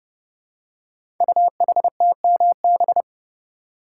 Morse code, heard as a keyed tone, about 30 wpm, 690 Hz. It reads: UHTMB
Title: Code Groups - 5 Letter:Numbers - 30wpm